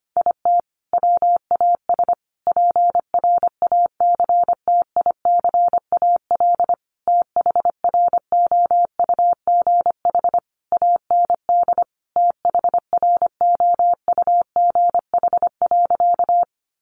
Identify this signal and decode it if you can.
Morse code, heard as a keyed tone, about 25 wpm, 700 Hz. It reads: IT WAS PRACTICAL THROUGH AND THROUGH.
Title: Sentences from Top 1400 Words 25wpm